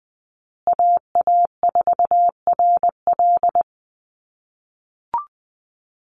Morse code, heard as a keyed tone, about 20 wpm, 700 Hz. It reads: AA4RL
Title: USA Call Signs Repeated 2x 20wpm